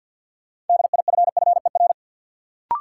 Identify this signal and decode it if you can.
Morse code, heard as a keyed tone, about 53 wpm, 700 Hz. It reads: DIFFER
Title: Sentences from Top 1100 Words - New Word Drill Alt - 50wpm